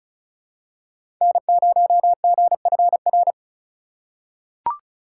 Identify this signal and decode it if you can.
Morse code, heard as a keyed tone, about 36 wpm, 700 Hz. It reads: N0GFR
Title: USA Call Signs - 5 Characters - 4x Character Spacing - 35wpm